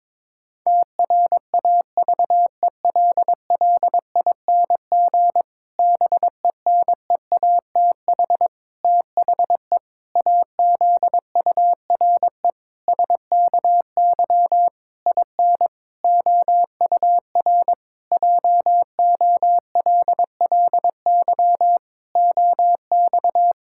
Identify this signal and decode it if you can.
Morse code, heard as a keyed tone, about 22 wpm, 700 Hz. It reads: TRAVELLING BENEATH THE AZURE SKY IN OUR JOLLY OX
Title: Pangram Challenge v2 - 22wpm